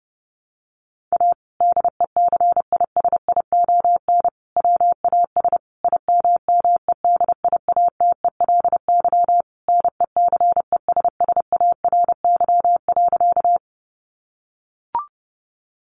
Morse code, heard as a keyed tone, about 30 wpm, 700 Hz. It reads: A DECISION WAS IMMEDIATELY NECESSARY.
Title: Sentences from Top 1100 Words 30wpm